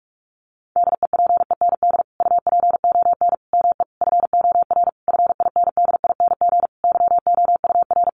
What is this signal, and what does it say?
Morse code, heard as a keyed tone, about 44 wpm, 700 Hz. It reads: DEPEND UPON ME FOR FINDING YOUR
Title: Sentences from Top 1300 Words 45wpm